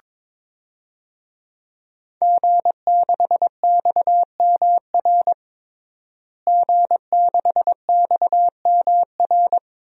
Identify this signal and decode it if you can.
Morse code, heard as a keyed tone, about 22 wpm, 700 Hz. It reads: G6XMR G6XMR
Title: DXCC-100 Call Signs - 5 Characters - Repeated 2x - 22wpm